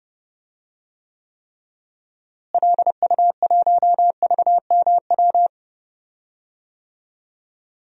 Morse code, heard as a keyed tone, about 31 wpm, 700 Hz. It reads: LU1VMW
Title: DXCC-100 Call Signs - 6 Characters - Repeated 3x - 30wpm